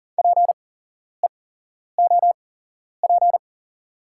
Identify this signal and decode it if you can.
Morse code, heard as a keyed tone, about 37 wpm, spaced as for 11 wpm, 700 Hz. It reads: PEOP
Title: 8x Character Spacing - Top 300 Words - 40wpm